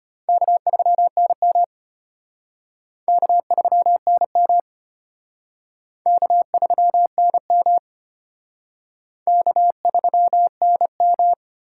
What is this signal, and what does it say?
Morse code, heard as a keyed tone, about 31 wpm, 700 Hz. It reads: K3NM K3NM K3NM K3NM
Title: Speed Racing - USA 1x2 Call Signs 25wpm